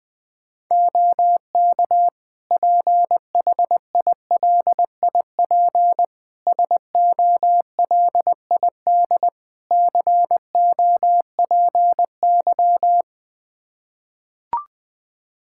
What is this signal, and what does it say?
Morse code, heard as a keyed tone, about 21 wpm, 700 Hz. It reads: OK PHILIP SOLID COPY
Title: K1USN SST QSO Practice 20wpm